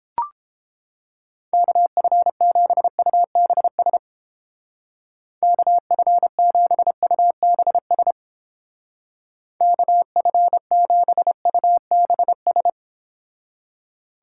Morse code, heard as a keyed tone, about 29 wpm, 700 Hz. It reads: KF7UBS KF7UBS KF7UBS